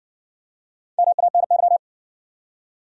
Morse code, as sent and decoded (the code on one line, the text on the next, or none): -- .- -. -.--
MANY